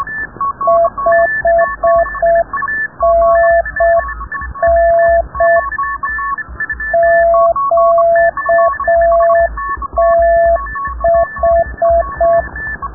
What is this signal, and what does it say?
Morse code, heard as a keyed tone, about 6 wpm, 690 Hz. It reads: 5NN QTH
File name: rubberst01 30 noise